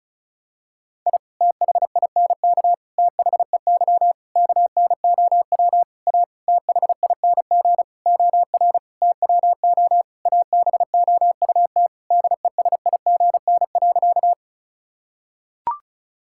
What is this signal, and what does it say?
Morse code, heard as a keyed tone, about 34 wpm, 700 Hz. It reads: I THINK THEY KNOW A THING OR TWO ABOUT DESIGN.
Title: Sentences from Top 1000 Words - Review Part 5 - 35wpm